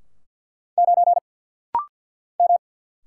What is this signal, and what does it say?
Morse code, heard as a keyed tone, about 50 wpm, 700 Hz. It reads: 9 M
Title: Single Letter-Number Alt - Rapid-Fire 50wpm